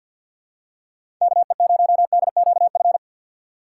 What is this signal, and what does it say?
Morse code, heard as a keyed tone, about 54 wpm, 700 Hz. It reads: K E 0 D Q F
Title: USA Call Signs - 6 Characters - 8x Character Spacing - 50wpm